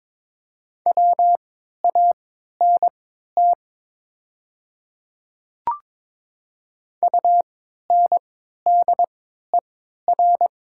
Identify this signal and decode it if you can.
Morse code, heard as a keyed tone, about 22 wpm, spaced as for 12 wpm, 700 Hz. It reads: WANT UNDER
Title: 3x Character Spacing - Top 200 Words - 22wpm